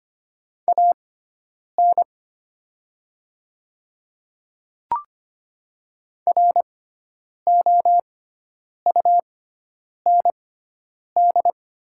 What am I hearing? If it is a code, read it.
Morse code, heard as a keyed tone, about 25 wpm, spaced as for 9 wpm, 700 Hz. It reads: AN ROUND
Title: 6x Character Spacing - Top 300 Words - 25wpm